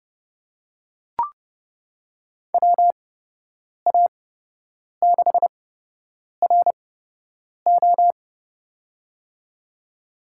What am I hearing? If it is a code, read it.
Morse code, heard as a keyed tone, about 30 wpm, spaced as for 8 wpm, 700 Hz. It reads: WA6RO
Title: USA Call Signs - 8x Character Spacing - 30wpm